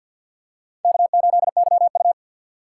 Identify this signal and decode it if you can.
Morse code, heard as a keyed tone, about 47 wpm, 700 Hz. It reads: K8YU
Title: USA Call Signs - 4 Characters - 6x Character Spacing - 50wpm